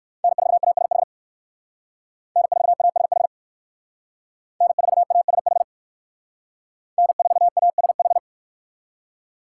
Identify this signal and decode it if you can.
Morse code, heard as a keyed tone, about 60 wpm, 700 Hz. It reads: N4ASH N4ASH N4ASH N4ASH